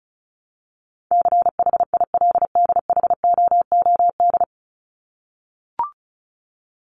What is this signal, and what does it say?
Morse code, heard as a keyed tone, about 34 wpm, 700 Hz. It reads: CHILDHOOD
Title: Top 1751-2000 Words 35wpm